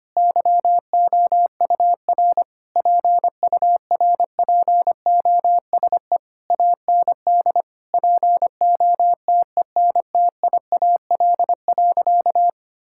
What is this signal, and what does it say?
Morse code, heard as a keyed tone, about 25 wpm, 700 Hz. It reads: YOUR PURPOSE AND POTENTIAL.
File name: practice / 25wpm / Sentences with 9-Word Phrases - 25wpm